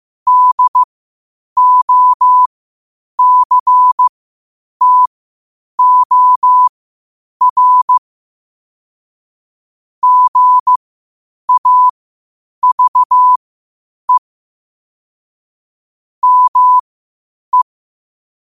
Morse code, heard as a keyed tone, about 15 wpm, spaced as for 8 wpm, 1000 Hz. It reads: DOCTOR GAVE ME
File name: qotdfiles / 8wpm